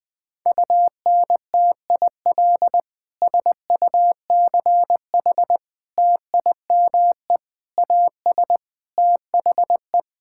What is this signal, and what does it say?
Morse code, heard as a keyed tone, about 20 wpm, 700 Hz. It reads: UNTIL SUCH TIME AS THE
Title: Sets of 9 Words - 20wpm